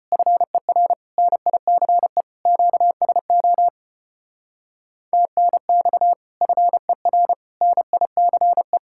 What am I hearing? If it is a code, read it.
Morse code, heard as a keyed tone, about 32 wpm, 700 Hz. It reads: FER NICE QSO TNX FER NICE